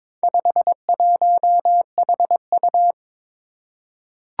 Morse code, heard as a keyed tone, about 22 wpm, 700 Hz. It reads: 51HU
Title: Four Challenging Letter-Number Combinations 22wpm